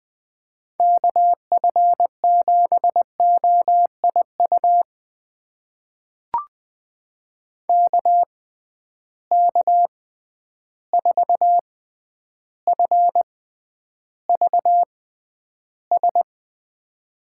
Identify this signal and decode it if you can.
Morse code, heard as a keyed tone, about 20 wpm, spaced as for 10 wpm, 700 Hz. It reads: KF7OIU KK4FVS
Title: USA Call Signs - 6 Characters - 6x Character Spacing - 20wpm